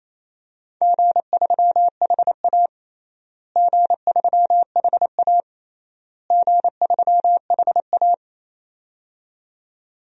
Morse code, heard as a keyed tone, about 28 wpm, 700 Hz. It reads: G3HA G3HA G3HA
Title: DXCC-100 Call Signs - 4 Characters - Repeated 3x - 28wpm